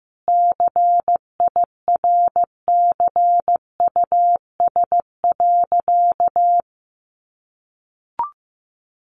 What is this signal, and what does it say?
Morse code, heard as a keyed tone, about 15 wpm, 700 Hz. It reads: CIRCUS.
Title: Easy Phrases 15wpm